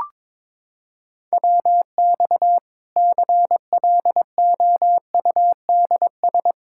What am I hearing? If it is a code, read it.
Morse code, heard as a keyed tone, about 22 wpm, 700 Hz. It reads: WX CLOUDS
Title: Practice QSO - Exchange 2 - Info - 22wpm